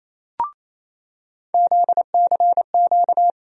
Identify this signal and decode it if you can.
Morse code, heard as a keyed tone, about 28 wpm, 700 Hz. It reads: ZCQ
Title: USA Call Sign Postfixes - 3 Letters 28wpm